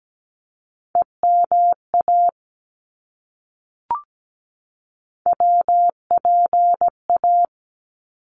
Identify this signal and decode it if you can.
Morse code, heard as a keyed tone, about 17 wpm, 700 Hz. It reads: EMA WPA
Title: ARRL Sections 17wpm